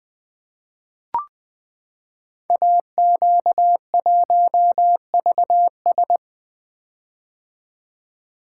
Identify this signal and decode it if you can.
Morse code, heard as a keyed tone, about 20 wpm, 700 Hz. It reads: AQ1VS
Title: Code Groups - 5 Letter:Numbers - 20wpm